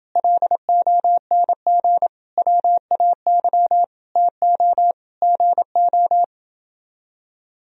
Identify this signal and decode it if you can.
Morse code, heard as a keyed tone, about 28 wpm, 700 Hz. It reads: LONG WAY TO GO